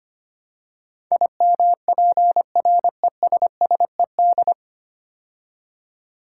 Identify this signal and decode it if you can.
Morse code, heard as a keyed tone, about 25 wpm, 700 Hz. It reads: IMPRESSED
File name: practice / 25wpm / 9 Letter Words - First 3 Letters Given - 25wpm